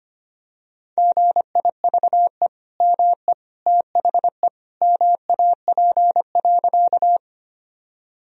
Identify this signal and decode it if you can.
Morse code, heard as a keyed tone, about 25 wpm, 700 Hz. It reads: GIVE ME THE MAP.
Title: Sentences from Top 1000 Words - Review Part 2 - 25wpm